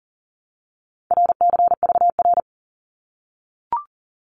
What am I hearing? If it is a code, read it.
Morse code, heard as a keyed tone, about 40 wpm, 700 Hz. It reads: RCVR
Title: Abbreviations Repeated 3x - 40wpm